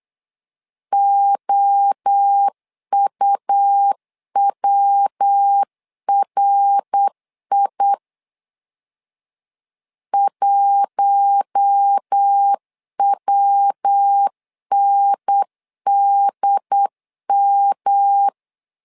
Morse code, heard as a keyed tone, about 8 wpm, 800 Hz. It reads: OUWRI 1WNDM
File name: LEZIONI / 4 Q1 8wpm 3035